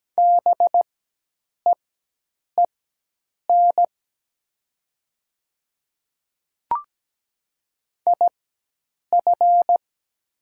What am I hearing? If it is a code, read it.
Morse code, heard as a keyed tone, about 17 wpm, spaced as for 8 wpm, 700 Hz. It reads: BEEN IF
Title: 4x Character Spacing - Top 100 Words - 17wpm